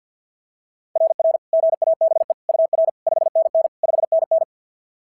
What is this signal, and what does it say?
Morse code, heard as a keyed tone, about 52 wpm, 690 Hz. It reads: RR GABE UR 5NN 5NN
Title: SOTA QSO Practice 50wpm